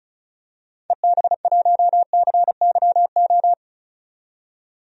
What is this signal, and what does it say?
Morse code, heard as a keyed tone, about 36 wpm, 700 Hz. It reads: EB1CYO
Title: DXCC-100 Call Signs - 6 Characters - 35wpm